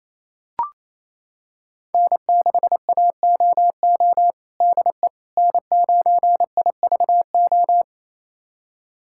Morse code, heard as a keyed tone, about 28 wpm, 700 Hz. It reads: N6AOO DE N9IVO
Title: USA Callsign DE Callsign - 5 Characters - 28wpm